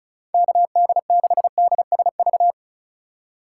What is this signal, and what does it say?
Morse code, heard as a keyed tone, about 36 wpm, 700 Hz. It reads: KD6DSV
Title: USA Call Signs - 6 Characters - 2x Character Spacing - 35wpm